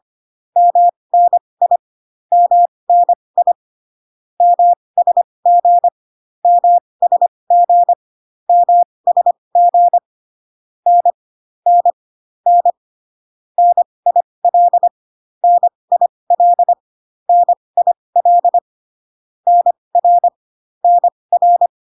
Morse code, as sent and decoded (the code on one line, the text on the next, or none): -- -. .. / -- -. .. / -- ... --. / -- ... --. / -- ... --. / -. / -. / -. / -. .. .-.. / -. .. .-.. / -. .. .-.. / -. .-. / -. .-.
MNI MNI MSG MSG MSG N N N NIL NIL NIL NR NR